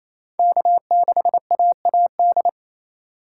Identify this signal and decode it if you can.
Morse code, heard as a keyed tone, about 28 wpm, 700 Hz. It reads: K6AAD
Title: USA Call Signs - 5 Characters - 2x Character Spacing - 28wpm